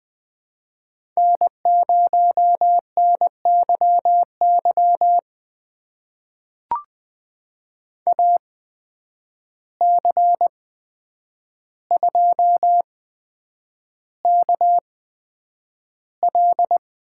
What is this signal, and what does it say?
Morse code, heard as a keyed tone, about 20 wpm, spaced as for 9 wpm, 700 Hz. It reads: N0NYY AC2KL
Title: USA Call Signs - 5 Characters - 8x Character Spacing - 20wpm